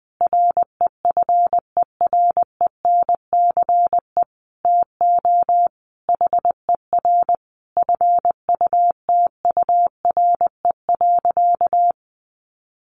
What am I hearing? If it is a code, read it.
Morse code, heard as a keyed tone, about 20 wpm, 700 Hz. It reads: REFERENCE TO HER FUTURE.
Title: Sentences from Top 1200 Words 20wpm